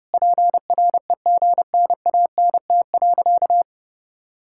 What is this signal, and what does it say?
Morse code, heard as a keyed tone, about 30 wpm, 700 Hz. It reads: PREGNANT.
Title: Sentences from Top 1000 Words - Review Part 6 - 30wpm